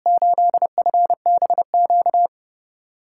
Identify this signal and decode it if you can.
Morse code, heard as a keyed tone, about 30 wpm, 700 Hz. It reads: 8FBQ
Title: USA Call Signs - 6x Character Spacing - 30wpm